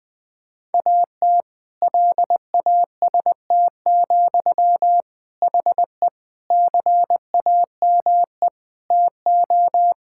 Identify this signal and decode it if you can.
Morse code, heard as a keyed tone, about 21 wpm, 700 Hz. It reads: AT LAST, HE CAME TO
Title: Sentences from Top 1600 Words 20wpm